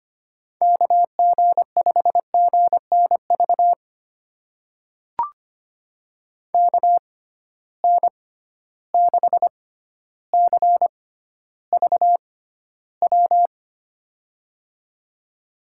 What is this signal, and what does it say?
Morse code, heard as a keyed tone, about 25 wpm, spaced as for 13 wpm, 700 Hz. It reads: KG5GNV KN6CVW
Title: USA Call Signs - 6 Characters - 6x Character Spacing - 25wpm